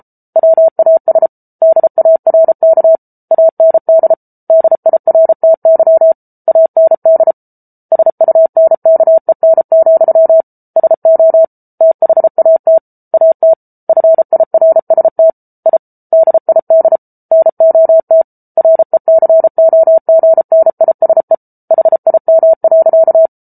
Morse code, heard as a keyed tone, about 34 wpm, 650 Hz. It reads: WAS DARK AND DIRTY AND SUNKEN, SO THAT AT FIRST I DID NOT RECOGNISE HIM.